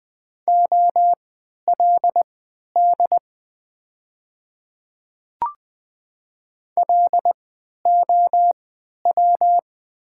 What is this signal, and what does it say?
Morse code, heard as a keyed tone, about 20 wpm, spaced as for 11 wpm, 700 Hz. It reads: OLD LOW
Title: 3x Character Spacing - Top 300 Words - 20wpm